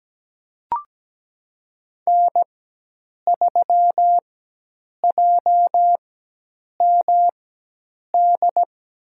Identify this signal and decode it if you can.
Morse code, heard as a keyed tone, about 17 wpm, spaced as for 8 wpm, 700 Hz. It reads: N3JMD
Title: USA Call Signs - 5 Characters - 4x Character Spacing - 17wpm